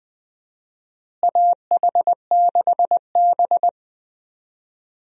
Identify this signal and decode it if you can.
Morse code, heard as a keyed tone, about 20 wpm, 700 Hz. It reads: AH6B
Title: USA Call Signs - 4 Characters - Repeated 2x - 20wpm